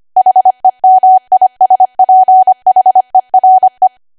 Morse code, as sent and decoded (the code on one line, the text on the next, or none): .... -- .. ... .--. .... .-. .
HEMISPHERE